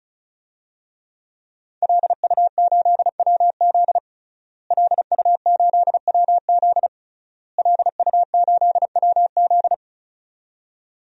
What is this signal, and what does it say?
Morse code, heard as a keyed tone, about 36 wpm, 700 Hz. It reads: LU8WZ LU8WZ LU8WZ
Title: DXCC-100 Call Signs - 5 Characters - Repeated 3x - 35wpm